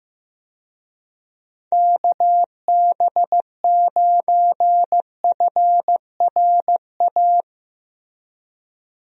Morse code, heard as a keyed tone, about 15 wpm, 700 Hz. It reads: KB9FRA
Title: USA Call Signs - 6 Characters - Repeated 3x - 15wpm